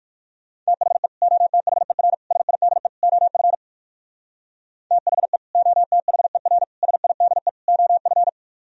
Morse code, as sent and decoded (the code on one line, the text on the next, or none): - .... / --- - .... .-. / ... .. -.. / --- ..-. / - .... / --- - .... .-. / ... .. -.. / --- ..-.
THE OTHER SIDE OF THE OTHER SIDE OF